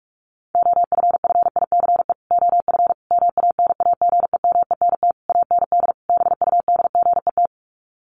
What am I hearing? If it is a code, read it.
Morse code, heard as a keyed tone, about 44 wpm, 700 Hz. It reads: OFFICE OF MANAGEMENT AND BUDGET
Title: Sets of 5 Words Encore2 45wpm